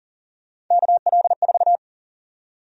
Morse code, heard as a keyed tone, about 40 wpm, 700 Hz. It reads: KL4
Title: USA Call Sign Prefixes - Repeated 2x - 40wpm